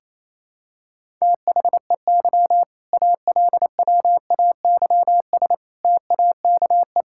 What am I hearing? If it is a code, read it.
Morse code, heard as a keyed tone, about 28 wpm, 700 Hz. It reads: THEY ALWAYS TAKE
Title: Sentences from Top 1000 Words - Review Part 6 - 28wpm